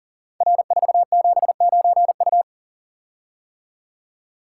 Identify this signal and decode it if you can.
Morse code, heard as a keyed tone, about 40 wpm, 700 Hz. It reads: R479U
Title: Code Groups - 5 Letter:Numbers - 40wpm